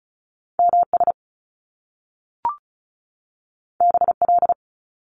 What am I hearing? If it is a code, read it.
Morse code, heard as a keyed tone, about 36 wpm, 700 Hz. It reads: MS BL
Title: Top Two Letter Combinations in Words 35wpm